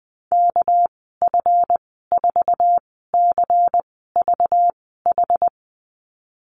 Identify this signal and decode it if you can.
Morse code, heard as a keyed tone, about 20 wpm, spaced as for 14 wpm, 700 Hz. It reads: KF4CVH
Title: 245 Lesson 39g - Letter z - Call Signs 20wpm